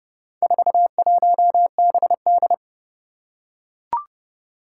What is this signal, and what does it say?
Morse code, heard as a keyed tone, about 32 wpm, 700 Hz. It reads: 41BD
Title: Four Challenging Letter-Number Combinations 30wpm